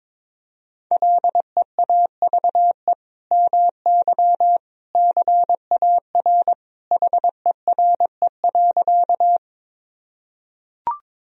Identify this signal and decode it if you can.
Morse code, heard as a keyed tone, about 22 wpm, 700 Hz. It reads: LEAVE MY CAR HERE.
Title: Sentences from Top 1000 Words - Review Part 1 - 22wpm